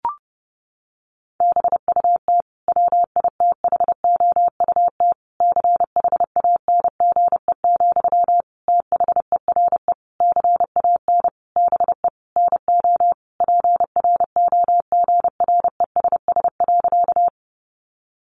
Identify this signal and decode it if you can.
Morse code, heard as a keyed tone, about 30 wpm, 700 Hz. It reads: BUT WITHOUT CHANGE, THERE CAN BE NO PROGRESS.
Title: Sentences from Top 1400 Words 30wpm